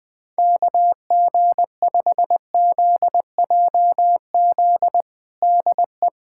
Morse code, heard as a keyed tone, about 20 wpm, 700 Hz. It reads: KG5ZJZ DE
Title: USA Callsign DE Callsign - 6 Characters - 20wpm